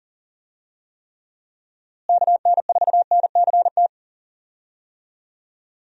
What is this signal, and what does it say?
Morse code, heard as a keyed tone, about 40 wpm, 700 Hz. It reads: KN4NCT
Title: USA Call Signs - 6 Characters - Repeated 2x - 40wpm